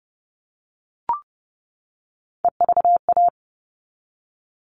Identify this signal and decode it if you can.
Morse code, heard as a keyed tone, about 30 wpm, 700 Hz. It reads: EVA
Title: All Cut Numbers - 3-Digit 30wpm